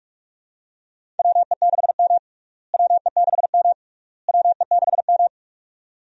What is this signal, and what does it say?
Morse code, heard as a keyed tone, about 46 wpm, 700 Hz. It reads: WE6M WE6M WE6M
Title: USA Call Signs - 4 Characters - Repeated 3x - 45wpm